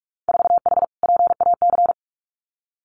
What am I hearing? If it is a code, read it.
Morse code, heard as a keyed tone, about 46 wpm, 700 Hz. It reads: 4H PAC
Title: Winter Field Day Exchanges Encore 45wpm